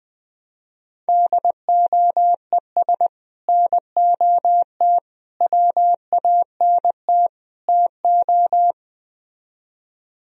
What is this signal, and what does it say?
Morse code, heard as a keyed tone, about 20 wpm, 700 Hz. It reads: DOES NOT WANT TO